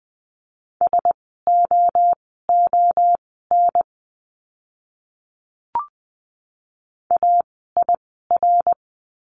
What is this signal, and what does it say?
Morse code, heard as a keyed tone, about 20 wpm, spaced as for 15 wpm, 700 Hz. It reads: SOON AIR
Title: 054 Lesson 10f - Letter r - Words 20wpm